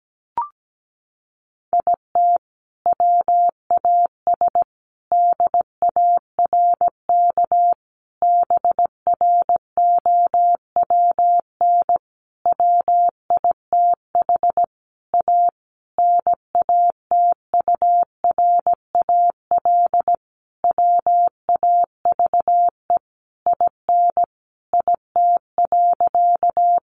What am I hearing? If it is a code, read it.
Morse code, heard as a keyed tone, about 17 wpm, 700 Hz. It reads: IT WAS DARK BROWN WITH A NATURAL WAVE IN IT.
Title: Sentences from Top 1400 Words 17wpm